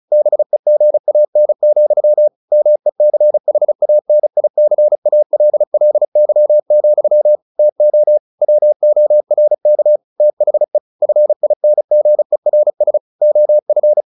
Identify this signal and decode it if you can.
Morse code, heard as a keyed tone, about 34 wpm, 600 Hz. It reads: BEGAN, MECHANICALLY, TO WORK THE FINGERS OF
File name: the old man and the sea / the old man and the sea 1 35wpm